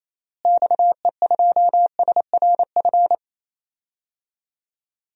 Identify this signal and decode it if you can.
Morse code, heard as a keyed tone, about 28 wpm, 700 Hz. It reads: XE2SRF
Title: DXCC-100 Call Signs - 6 Characters - Repeated 3x - 28wpm